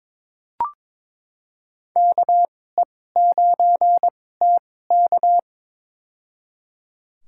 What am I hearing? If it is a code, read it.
Morse code, heard as a keyed tone, about 22 wpm, spaced as for 16 wpm, 700 Hz. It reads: KE9TK
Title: USA Call Signs - 2x Character Spacing - 22wpm